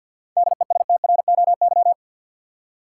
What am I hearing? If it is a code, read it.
Morse code, heard as a keyed tone, about 51 wpm, 700 Hz. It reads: DESTROY
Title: Seven Letter Words - 50wpm